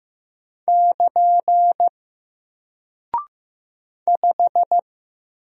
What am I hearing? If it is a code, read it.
Morse code, heard as a keyed tone, about 15 wpm, 700 Hz. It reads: <KN> 5
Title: Single Character (Letter:Number:Punctuation:Prosign) Alt 15wpm